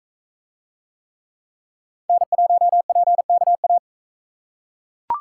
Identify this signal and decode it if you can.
Morse code, heard as a keyed tone, about 41 wpm, 700 Hz. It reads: N1PKA